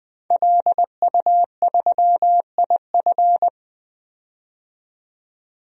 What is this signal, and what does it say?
Morse code, heard as a keyed tone, about 21 wpm, 700 Hz. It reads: LU3IF
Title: DXCC-100 Call Signs - 5 Characters - 20wpm